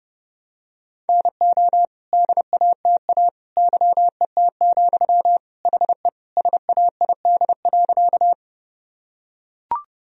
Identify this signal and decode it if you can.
Morse code, heard as a keyed tone, about 30 wpm, 700 Hz. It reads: NO DATA YET, HE SAID.
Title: Sentences from Top 1000 Words - Review Part 3 - 30wpm